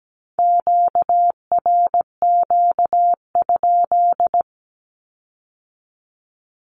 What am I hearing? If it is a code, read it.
Morse code, heard as a keyed tone, about 17 wpm, 700 Hz. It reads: QRQ?
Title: Q-Codes with Spoken Meaning - 17wpm